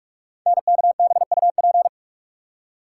Morse code, heard as a keyed tone, about 44 wpm, 700 Hz. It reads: NKBUP
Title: Code Groups - 5 Letters - 45wpm